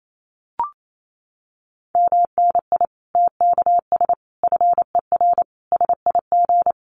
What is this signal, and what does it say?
Morse code, heard as a keyed tone, about 28 wpm, 700 Hz. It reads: MNI TKS FER SIG
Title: Common QSO Phrases 28wpm